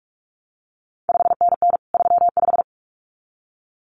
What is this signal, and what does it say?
Morse code, heard as a keyed tone, about 46 wpm, 700 Hz. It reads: 5NN 35
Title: CQ WW DX Contest Exchange Encore - 45wpm